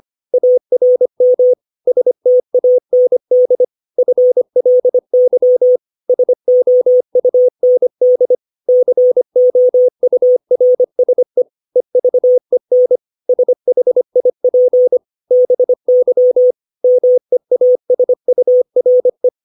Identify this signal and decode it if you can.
Morse code, heard as a keyed tone, about 25 wpm, 500 Hz. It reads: ARM STAND FLY SOUND COURSE EVEN SHIP BY MEASURE
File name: 632 words / 632 Words 25WPM